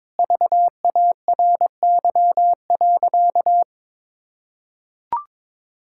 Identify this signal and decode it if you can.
Morse code, heard as a keyed tone, about 22 wpm, 700 Hz. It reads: VARY.
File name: practice / 22wpm / Sentences with 2-Word Phrases - 22wpm